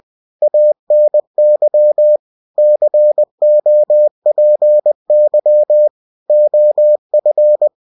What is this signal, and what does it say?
Morse code, heard as a keyed tone, about 20 wpm, 600 Hz. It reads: ANY COPY OF
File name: cw-books / winnie-the-pooh / winnie-20wpm-page-0115